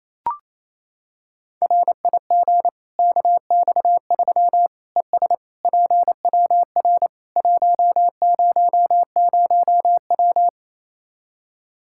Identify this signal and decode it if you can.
Morse code, heard as a keyed tone, about 28 wpm, 700 Hz. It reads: RIG KX3 ES PWR 100W
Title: K1USN SST QSO Practice 28wpm